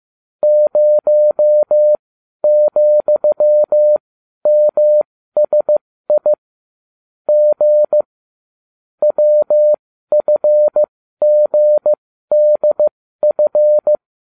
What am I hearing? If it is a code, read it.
Morse code, heard as a keyed tone, about 15 wpm, spaced as for 12 wpm, 600 Hz. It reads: 0,MSI G WFGDF